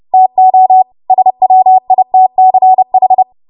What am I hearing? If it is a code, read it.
Morse code, heard as a keyed tone, about 30 wpm, 750 Hz. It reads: TO SWITCH
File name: Morse / 30WPM